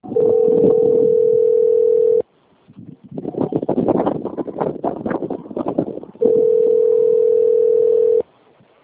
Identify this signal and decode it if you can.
Ringback tone (North America), 440 + 480 Hz.